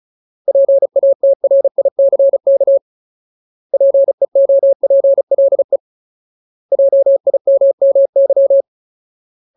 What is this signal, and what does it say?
Morse code, heard as a keyed tone, about 34 wpm, 550 Hz. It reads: PATRICK PEOPLE JIMMY
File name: Ultimate Word List / 35WPM Ultimate